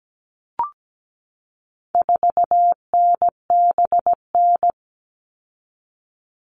Morse code, heard as a keyed tone, about 17 wpm, 700 Hz. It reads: 4NBN
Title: All Cut Numbers - 4-Digit 17wpm